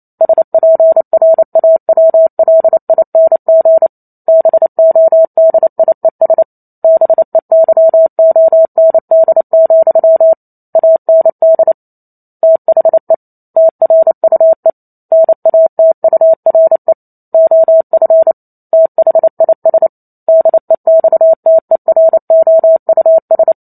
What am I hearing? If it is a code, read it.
Morse code, heard as a keyed tone, about 28 wpm, 650 Hz. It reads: SPRAWLING BODIES BEYOND, AND THE TRUE NATURE OF THIS DEXTEROUS